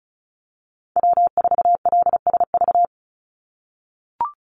Morse code, heard as a keyed tone, about 34 wpm, 700 Hz. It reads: W4LSV
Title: Call Signs - 35wpm